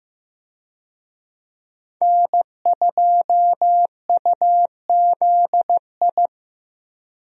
Morse code, heard as a keyed tone, about 16 wpm, 700 Hz. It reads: N2UZI
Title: USA Call Signs - 5 Characters - Repeated 2x - 15wpm